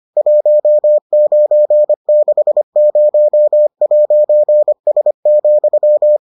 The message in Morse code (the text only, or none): 1960'S,